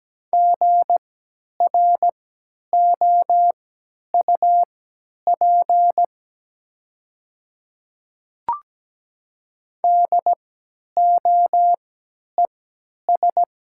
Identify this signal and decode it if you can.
Morse code, heard as a keyed tone, about 17 wpm, spaced as for 10 wpm, 700 Hz. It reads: GROUP DOES
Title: 3x Character Spacing - Top 300 Words - 17wpm